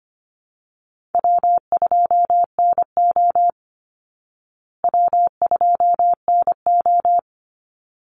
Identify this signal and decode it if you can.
Morse code, heard as a keyed tone, about 25 wpm, 700 Hz. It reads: W2NO W2NO
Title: Easy Call Signs Repeated 2x - 25wpm